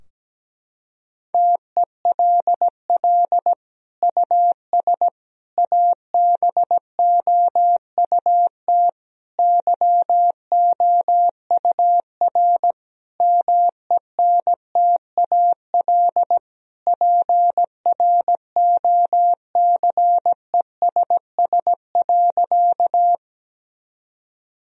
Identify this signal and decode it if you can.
Morse code, heard as a keyed tone, about 17 wpm, 700 Hz. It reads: TELL US ABOUT YOUR MENTAL PROCESS.
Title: Sentences from Top 1000 Words - Review Part 4 - 17wpm